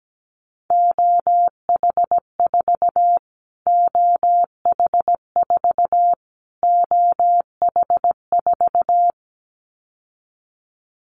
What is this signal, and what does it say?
Morse code, heard as a keyed tone, about 17 wpm, 700 Hz. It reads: OH4 OH4 OH4
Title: Callsign Prefixes for Top 25 DXCC - Repeated 3x - 17wpm